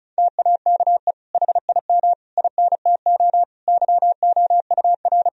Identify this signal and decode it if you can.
Morse code, heard as a keyed tone, about 34 wpm, 700 Hz. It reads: TAKE HIM INTO YOUR
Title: Sentences from Top 1000 Words - Review Part 4 - 35wpm